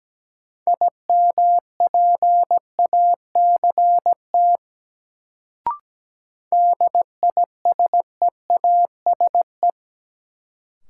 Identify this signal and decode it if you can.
Morse code, heard as a keyed tone, about 17 wpm, 700 Hz. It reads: IMPACT DISEASE
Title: Sentences from Top 1000 Words - Review Part 6 - New Word Drill - 17wpm